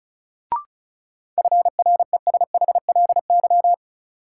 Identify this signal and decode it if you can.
Morse code, heard as a keyed tone, about 36 wpm, 700 Hz. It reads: FRESHLY
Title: Sets of 5 Words Encore - New Word Drill - 35wpm